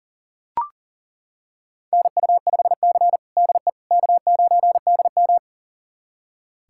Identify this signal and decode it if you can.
Morse code, heard as a keyed tone, about 40 wpm, 690 Hz. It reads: NU5C DE K9DM
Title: USA Callsign DE Callsign - 4 Characters - 40wpm